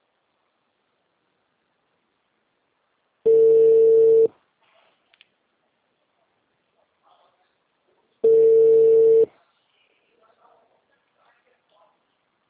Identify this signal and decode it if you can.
Ringback tone (China, 450 Hz), with speech.